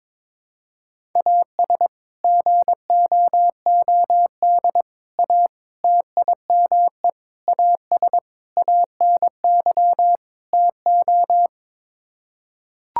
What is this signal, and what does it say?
Morse code, heard as a keyed tone, about 22 wpm, 700 Hz. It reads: AS GOOD A TIME AS ANY TO
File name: practice / 22wpm / Sets of 7 Words - 22wpm